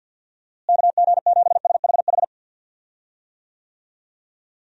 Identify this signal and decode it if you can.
Morse code, heard as a keyed tone, about 50 wpm, 700 Hz. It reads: KG7SHH
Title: USA Call Signs - 6 Characters - Repeated 3x - 50wpm